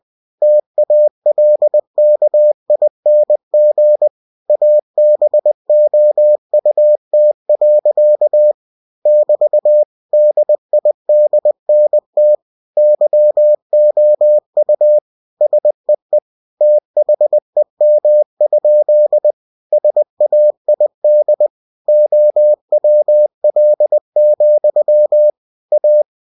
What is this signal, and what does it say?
Morse code, heard as a keyed tone, about 20 wpm, 600 Hz. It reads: TALKING ABOUT. <BT> DIDNT YOU SEE THEM? SAID OWL, A
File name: cw-books / winnie-the-pooh / winnie-20wpm-page-0031